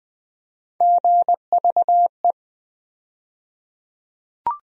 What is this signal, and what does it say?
Morse code, heard as a keyed tone, about 20 wpm, 700 Hz. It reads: GVE
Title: USA Call Sign Postfixes - 3 Letters 20wpm